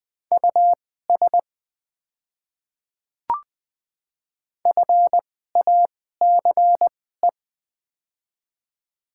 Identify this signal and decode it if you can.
Morse code, heard as a keyed tone, about 20 wpm, spaced as for 15 wpm, 690 Hz. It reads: US FACE
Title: 2x Character Spacing - Top 300 Words - 20wpm